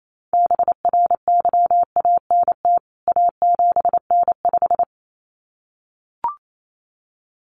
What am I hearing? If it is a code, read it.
Morse code, heard as a keyed tone, about 28 wpm, 700 Hz. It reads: BRYANT A7N5
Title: CWT Exchange 28wpm